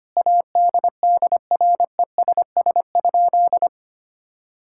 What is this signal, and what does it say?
Morse code, heard as a keyed tone, about 25 wpm, 700 Hz. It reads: ADDRESS?
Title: Sentences from Top 1000 Words - Review Part 6 - 25wpm